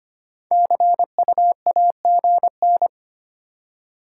Morse code, heard as a keyed tone, about 31 wpm, 700 Hz. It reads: CU AGN
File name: practice / 25wpm / Speed-Racing Top 20 QSO Elements - 25wpm